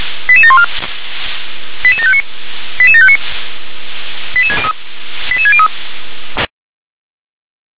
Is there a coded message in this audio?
Yes: five-tone selective call (ZVEI) from 0.3 to 5.7 s, heard as quick runs of five tones, tones of about 70 ms. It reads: 88714 88759 88759 88603